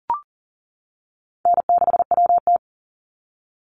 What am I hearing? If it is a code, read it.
Morse code, heard as a keyed tone, about 40 wpm, 700 Hz. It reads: N6WT
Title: NA SOTA Chaser Callsigns 40wpm